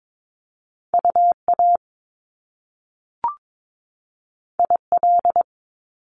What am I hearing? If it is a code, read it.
Morse code, heard as a keyed tone, about 22 wpm, 700 Hz. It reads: UA IL
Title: Top Two Letter Combinations in Words 22wpm